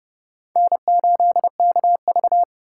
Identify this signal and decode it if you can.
Morse code, heard as a keyed tone, about 30 wpm, 700 Hz. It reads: N8KV
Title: USA Call Signs - 4 Characters - 4x Character Spacing - 30wpm